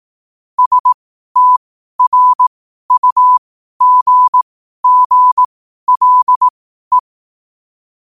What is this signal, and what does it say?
Morse code, heard as a keyed tone, about 18 wpm, spaced as for 12 wpm, 1000 Hz. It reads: STRUGGLE
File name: qotdfiles / 12wpm